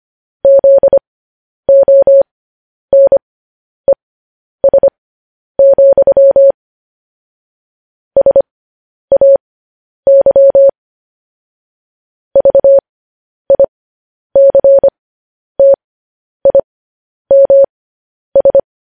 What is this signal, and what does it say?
Morse code, heard as a keyed tone, about 25 wpm, spaced as for 10 wpm, 550 Hz. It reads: ZONES, SAY VICTIMS
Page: news headlines in morse code at 10 wpm